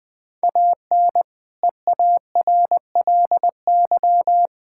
Morse code, heard as a keyed tone, about 20 wpm, 700 Hz. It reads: AN EARLY
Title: Difficult Sentences with Hints - 20wpm